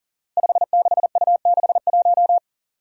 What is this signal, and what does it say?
Morse code, heard as a keyed tone, about 40 wpm, 700 Hz. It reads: 56U61
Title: Five Challenging Letter-Number Combinations 40wpm